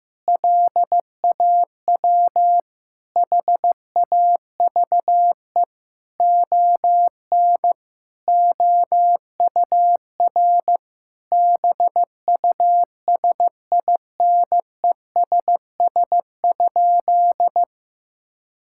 Morse code, heard as a keyed tone, about 15 wpm, 700 Hz. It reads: LAW HAVE ON OUR BUSINESS?
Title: Sentences from Top 1000 Words - Review Part 6 - 15wpm